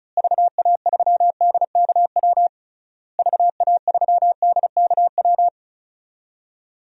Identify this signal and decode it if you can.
Morse code, heard as a keyed tone, about 36 wpm, 700 Hz. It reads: VA3DKW VA3DKW
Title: DXCC-100 Call Signs - 6 Characters - Repeated 3x - 35wpm